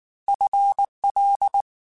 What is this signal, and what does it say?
Morse code, heard as a keyed tone, about 19 wpm, 800 Hz. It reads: FL
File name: FL / FL mp3s / FL Prefixes 3x2 @18